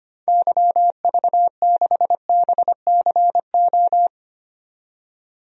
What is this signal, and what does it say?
Morse code, heard as a keyed tone, about 25 wpm, 700 Hz. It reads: YV6BCO
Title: DXCC-100 Call Signs - 6 Characters - 25wpm